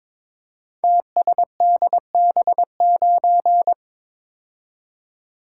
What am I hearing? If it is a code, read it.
Morse code, heard as a keyed tone, about 22 wpm, 700 Hz. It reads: TSDB9
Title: Code Groups - 5 Letter:Numbers - 22wpm